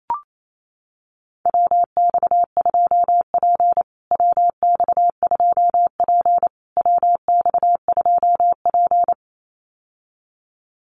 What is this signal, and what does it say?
Morse code, heard as a keyed tone, about 28 wpm, 700 Hz. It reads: WX2P WX2P WX2P
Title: USA Call Signs Repeated 3x 28wpm